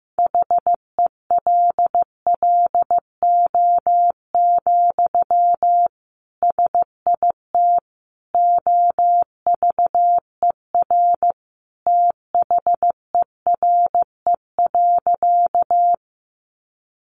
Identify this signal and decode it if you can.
Morse code, heard as a keyed tone, about 15 wpm, 700 Hz. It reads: HELLO, SIT OVER THERE.
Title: Sentences from Top 1100 Words 15wpm